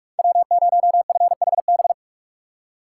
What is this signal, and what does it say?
Morse code, heard as a keyed tone, about 47 wpm, 700 Hz. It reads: W0FHB
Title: USA Call Signs - 5 Characters - 4x Character Spacing - 45wpm